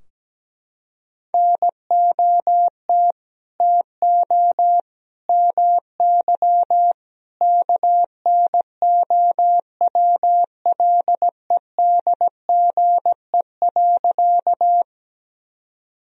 Morse code, heard as a keyed tone, about 17 wpm, 700 Hz. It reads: NOT TO MY KNOWLEDGE.
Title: Sentences from Top 1000 Words - Review Part 3 - 17wpm